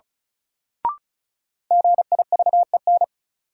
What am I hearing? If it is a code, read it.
Morse code, heard as a keyed tone, about 34 wpm, 700 Hz. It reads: GIVEN
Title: Sentences from Top 1400 Words - New Word Drill - 35wpm